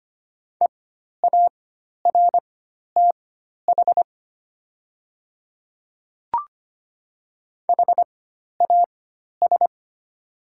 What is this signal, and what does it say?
Morse code, heard as a keyed tone, about 25 wpm, spaced as for 12 wpm, 700 Hz. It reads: EARTH HAS
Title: 4x Character Spacing - Top 300 Words - 25wpm